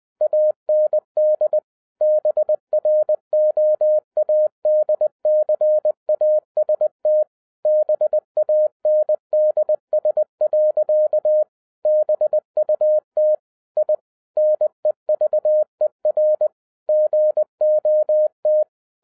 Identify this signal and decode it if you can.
Morse code, heard as a keyed tone, about 20 wpm, 610 Hz. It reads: AND BROADCAST BANDS. BUT I NEVER GOT
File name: 2011-07-mp3 / n08 A LOT HAS CHANGED 20wpm